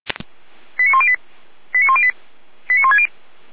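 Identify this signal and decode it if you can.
Five-tone selective call (ZVEI), heard as quick runs of five tones, tones of about 70 ms. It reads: 89108 89108 89160
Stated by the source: ZVEI five-tone selective call received on SDR.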